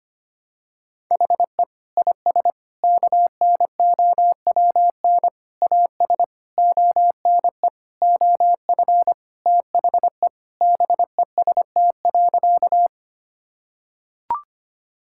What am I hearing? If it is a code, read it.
Morse code, heard as a keyed tone, about 25 wpm, 700 Hz. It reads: HE IS KNOWN AS ONE OF THE BEST.
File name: practice / 25wpm / Sentences with 4-Word Phrases - 25wpm